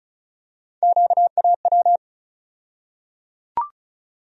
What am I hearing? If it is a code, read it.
Morse code, heard as a keyed tone, about 36 wpm, 700 Hz. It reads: QAW
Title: USA Call Sign Postfixes - 3 Letters 35wpm